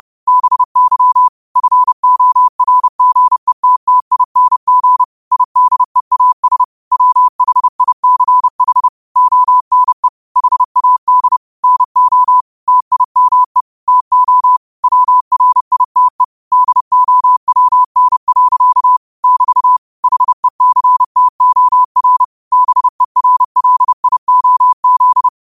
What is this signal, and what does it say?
Morse code, heard as a keyed tone, about 30 wpm, 1000 Hz. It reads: BY FORGETTING IDEAS WHICH ONE HAD NO TIME TO WRITE DOWN. <BT> HECTOR BERLIOZ